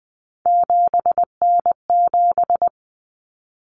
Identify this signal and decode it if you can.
Morse code, heard as a keyed tone, about 20 wpm, 700 Hz. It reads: 7N7
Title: Common Cut Numbers - 3-Digit 20wpm